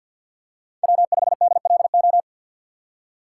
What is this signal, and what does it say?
Morse code, heard as a keyed tone, about 53 wpm, 700 Hz. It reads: W5DLO
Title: USA Call Signs - 5 Characters - 8x Character Spacing - 50wpm